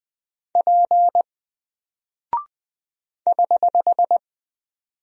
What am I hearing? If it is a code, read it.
Morse code, heard as a keyed tone, about 20 wpm, 700 Hz. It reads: P <HH>
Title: Single Character (Letter:Number:Punctuation:Prosign) 20wpm